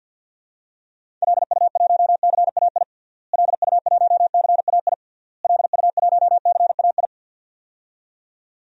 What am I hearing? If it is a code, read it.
Morse code, heard as a keyed tone, about 51 wpm, 700 Hz. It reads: LU1CAI LU1CAI LU1CAI
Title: DXCC-100 Call Signs - 6 Characters - Repeated 3x - 50wpm